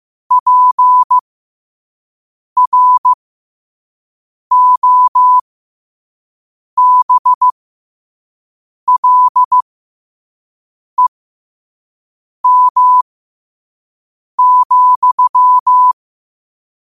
Morse code, heard as a keyed tone, about 15 wpm, spaced as for 5 wpm, 1000 Hz. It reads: PROBLEM,
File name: qotdfiles / 5wpm